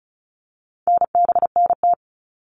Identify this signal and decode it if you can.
Morse code, heard as a keyed tone, about 34 wpm, 700 Hz. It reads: NBNT
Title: All Cut Numbers - 4-Digit 35wpm